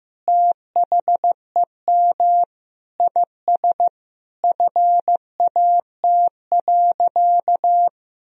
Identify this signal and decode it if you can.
Morse code, heard as a keyed tone, about 15 wpm, 700 Hz. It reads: THEM IS FAT.
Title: Sentences from Top 1000 Words - Review Part 4 - 15wpm